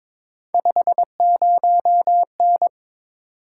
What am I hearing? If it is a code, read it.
Morse code, heard as a keyed tone, about 22 wpm, 700 Hz. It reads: 50N
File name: practice / 22wpm / Three Letter-Number Combinations - 22wpm